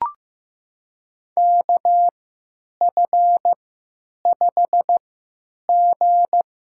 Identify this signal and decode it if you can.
Morse code, heard as a keyed tone, about 15 wpm, spaced as for 9 wpm, 700 Hz. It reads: KF5G
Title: USA Call Signs - 3x Character Spacing - 15wpm